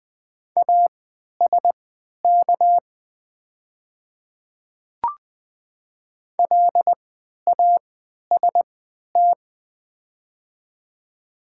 Morse code, heard as a keyed tone, about 20 wpm, spaced as for 11 wpm, 700 Hz. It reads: ASK LAST
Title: 3x Character Spacing - Top 500 Words - 20wpm